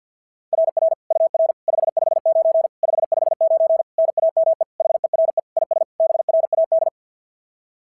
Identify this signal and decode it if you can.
Morse code, heard as a keyed tone, about 46 wpm, 660 Hz. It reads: RR UR 559 559 NAME HERE IS BRAD
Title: SOTA QSO Practice 50wpm